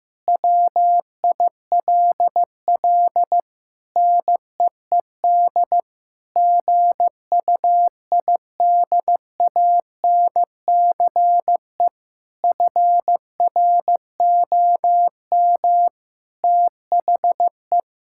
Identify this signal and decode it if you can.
Morse code, heard as a keyed tone, about 15 wpm, 700 Hz. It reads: WILL NEED GUIDANCE FROM THE
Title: Sentences from Top 1600 Words 15wpm